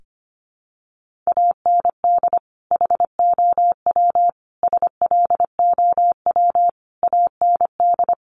Morse code, heard as a keyed tone, about 25 wpm, 700 Hz. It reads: AND HOW SLOW AND
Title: Sentences from Top 1500 Words 25wpm